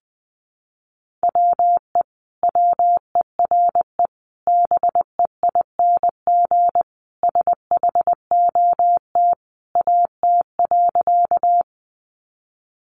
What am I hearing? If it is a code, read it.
Morse code, heard as a keyed tone, about 20 wpm, 700 Hz. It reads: WE WERE BEING SHOT AT.
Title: Sentences from Top 1100 Words 20wpm